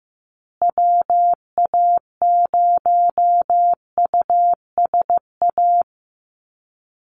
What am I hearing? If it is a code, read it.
Morse code, heard as a keyed tone, about 15 wpm, 700 Hz. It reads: WA0USA
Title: Call Signs Repeated 3x - 15wpm